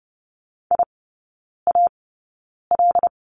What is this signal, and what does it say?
Morse code, heard as a keyed tone, about 30 wpm, spaced as for 9 wpm, 700 Hz. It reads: IAL
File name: 081 Lesson 14e - Number 2 - Three Characters 30wpm